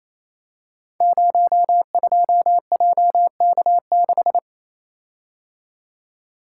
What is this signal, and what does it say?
Morse code, heard as a keyed tone, about 29 wpm, 700 Hz. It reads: 02JK6
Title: Code Groups - 5 Letter:Numbers - 28wpm